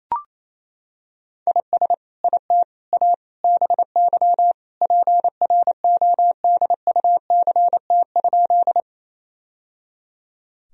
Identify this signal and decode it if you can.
Morse code, heard as a keyed tone, about 28 wpm, 700 Hz. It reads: IS IT A BY PRODUCT?
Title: Sentences from Top 1000 Words - Review Part 2 - 28wpm